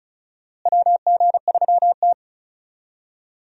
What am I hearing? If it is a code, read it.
Morse code, heard as a keyed tone, about 34 wpm, 700 Hz. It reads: WG3T
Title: USA Call Signs - 4 Characters - Repeated 2x - 35wpm